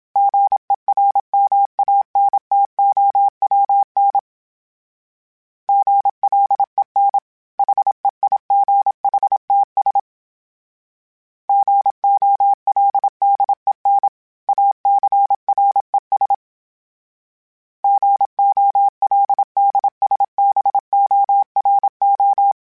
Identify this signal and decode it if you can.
Morse code, heard as a keyed tone, about 27 wpm, 800 Hz. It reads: GERMANTOWN GLEN HEIGHTS GOLDEN ACRES GOLDSBORO
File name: fromSMs / FromSMs mp3s / Cities@25